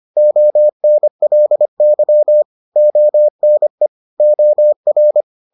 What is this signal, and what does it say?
Morse code, heard as a keyed tone, about 25 wpm, 600 Hz. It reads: ONLY ONE OR